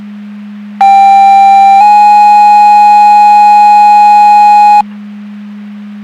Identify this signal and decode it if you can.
Two-tone page, heard as a one-second tone, then a three-second tone, about 790 then 830 Hz. It is heard from 0.8 to 4.8 s.
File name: Portage County 154.3850 Fire EMS Paging